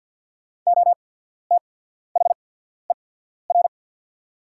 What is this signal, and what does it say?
Morse code, heard as a keyed tone, about 50 wpm, spaced as for 14 wpm, 690 Hz. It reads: OTHER
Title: 8x Character Spacing - Top 100 Words - 50wpm